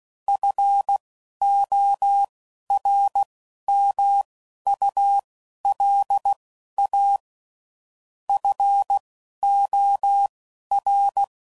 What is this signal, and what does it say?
Morse code, heard as a keyed tone, about 16 wpm, spaced as for 11 wpm, 800 Hz. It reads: FORMULA FOR
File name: CW Code Groups @10WPM / Technical Subjects / XC